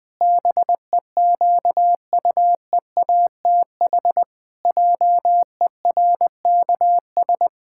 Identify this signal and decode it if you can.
Morse code, heard as a keyed tone, about 20 wpm, 700 Hz. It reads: BEQUEATH JERKS
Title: Pangram Challenge v2 - 20wpm